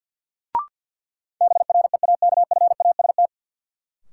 Morse code, heard as a keyed tone, about 53 wpm, 700 Hz. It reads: BREAKFAST
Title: Sentences from Top 1500 Words - New Word Drill - 50wpm